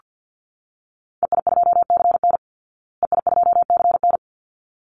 Morse code, heard as a keyed tone, about 48 wpm, 700 Hz. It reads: EI2CN EI2CN
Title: Call Signs Repeated 3x - 50wpm